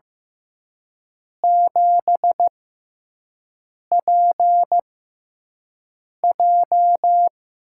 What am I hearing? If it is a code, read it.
Morse code, heard as a keyed tone, about 15 wpm, spaced as for 5 wpm, 700 Hz. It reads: 7PJ
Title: USA Call Signs - 4 Characters - 6x Character Spacing - 15wpm